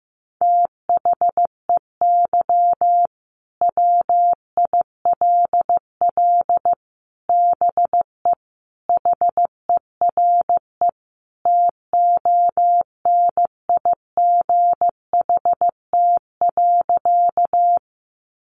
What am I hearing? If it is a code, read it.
Morse code, heard as a keyed tone, about 14 wpm, 700 Hz. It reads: THEY WILL BE HERE TONIGHT.
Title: Sentences from Top 600 Words 15wpm